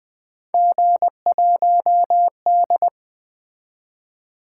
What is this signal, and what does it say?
Morse code, heard as a keyed tone, about 20 wpm, 700 Hz. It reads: G1D
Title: Three Letter-Number Combinations - 20wpm